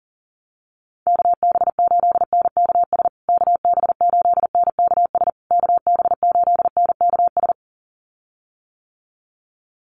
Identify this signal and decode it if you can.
Morse code, heard as a keyed tone, about 40 wpm, 700 Hz. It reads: KB8NKS KB8NKS KB8NKS
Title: USA Call Signs Repeated 3x 40wpm